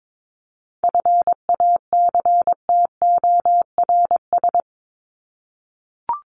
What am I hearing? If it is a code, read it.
Morse code, heard as a keyed tone, about 22 wpm, 700 Hz. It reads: FACTORS
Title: ES:S Suffix - Top 100 Words with Base Words 22wpm